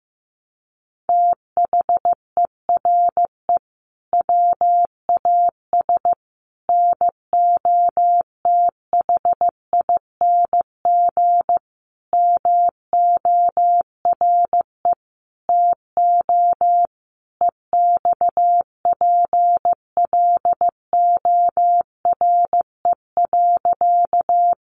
Morse code, heard as a keyed tone, about 15 wpm, 700 Hz. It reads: THERE WAS NOTHING MORE TO EXPLORE.
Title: Sentences from Top 1100 Words 15wpm